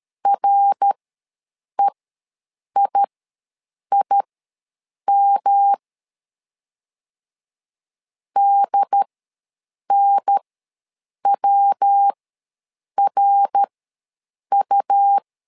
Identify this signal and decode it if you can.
Morse code, heard as a keyed tone, about 13 wpm, spaced as for 7 wpm, 800 Hz. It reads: REIIM DNWRU
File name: LEZIONI / 3 MWRO / 3 MWR ws 12wpm 1809